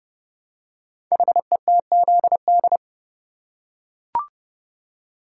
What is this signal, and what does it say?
Morse code, heard as a keyed tone, about 29 wpm, 700 Hz. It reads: HETZD
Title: Code Groups - 5 Letters - 30wpm